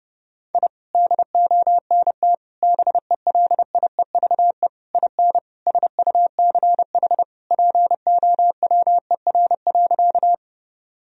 Morse code, heard as a keyed tone, about 30 wpm, 700 Hz. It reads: I DONT BELIEVE IN SUCH POWER.
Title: Sentences from Top 1000 Words - Review Part 2 - 30wpm